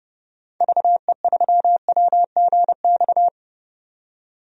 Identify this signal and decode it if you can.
Morse code, heard as a keyed tone, about 32 wpm, 700 Hz. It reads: VE3WGX
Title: DXCC-100 Call Signs - 6 Characters - 30wpm